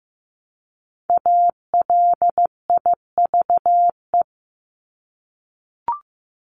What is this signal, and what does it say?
Morse code, heard as a keyed tone, about 15 wpm, 700 Hz. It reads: ALIVE
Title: Top 1751-2000 Words 15wpm